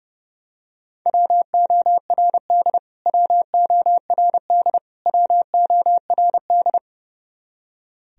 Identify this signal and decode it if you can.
Morse code, heard as a keyed tone, about 32 wpm, 700 Hz. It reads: WORD WORD WORD
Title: Top 100 Words Repeated 30wpm